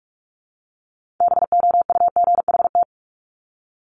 Morse code, heard as a keyed tone, about 46 wpm, 700 Hz. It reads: BOUGHT
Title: Multi-letter Phonemes Repeated 3x 45wpm